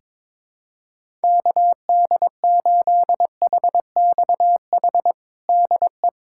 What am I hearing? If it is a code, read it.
Morse code, heard as a keyed tone, about 22 wpm, 700 Hz. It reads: KD8HXH DE
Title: USA Callsign DE Callsign 22wpm